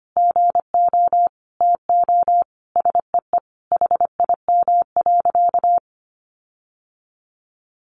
Morse code, heard as a keyed tone, about 25 wpm, 700 Hz. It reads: GO TO SEE HIM.
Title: Sentences from Top 1500 Words 25wpm